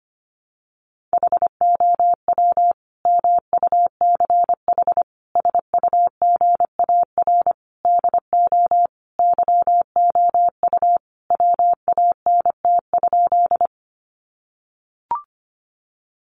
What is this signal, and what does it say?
Morse code, heard as a keyed tone, about 25 wpm, 700 Hz. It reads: HOW MUCH SUGAR DO YOU WANT?
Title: Sentences from Top 1200 Words 25wpm